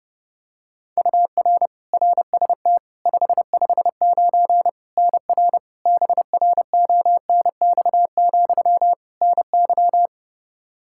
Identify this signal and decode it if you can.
Morse code, heard as a keyed tone, about 31 wpm, 700 Hz. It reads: UR RST 559 NR BRONX, NY
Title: Practice QSO - Exchange 1 - Beginning - 30wpm